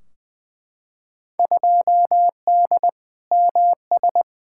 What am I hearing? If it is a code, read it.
Morse code, heard as a keyed tone, about 20 wpm, 700 Hz. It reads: 2D MS
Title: Field Day Exchanges Encore - 20wpm